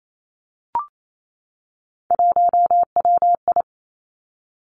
Morse code, heard as a keyed tone, about 28 wpm, 700 Hz. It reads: 1WI
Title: Three Challenging Letter-Number Combinations 28wpm